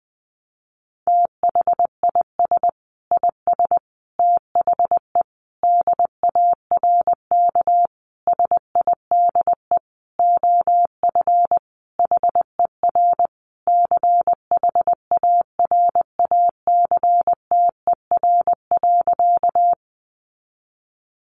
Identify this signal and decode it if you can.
Morse code, heard as a keyed tone, about 19 wpm, 700 Hz. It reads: THIS IS THE DARK SIDE OF HER CHARACTER.
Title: Sentences from Top 1200 Words 20wpm